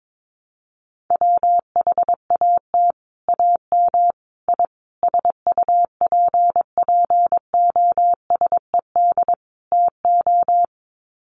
Morse code, heard as a keyed tone, about 22 wpm, 700 Hz. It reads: WHAT AM I SUPPOSED TO
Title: Sets of 5 Words Encore2 22wpm